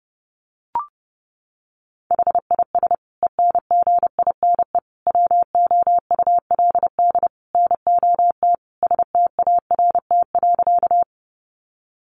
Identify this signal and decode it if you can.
Morse code, heard as a keyed tone, about 30 wpm, 700 Hz. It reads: HIS ENGINE WOULD NOT START.
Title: Sentences from Top 1500 Words 30wpm